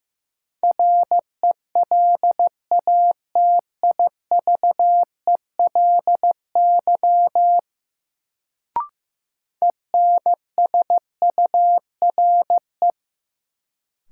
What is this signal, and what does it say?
Morse code, heard as a keyed tone, about 15 wpm, 700 Hz. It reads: RELATIVELY ENSURE
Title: Sentences from Top 1100 Words - New Word Drill - 15wpm